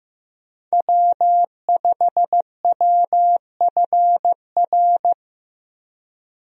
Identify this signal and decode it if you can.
Morse code, heard as a keyed tone, about 15 wpm, 700 Hz. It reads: W5WFR
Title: USA Call Signs - 5 Characters - Repeated 3x - 15wpm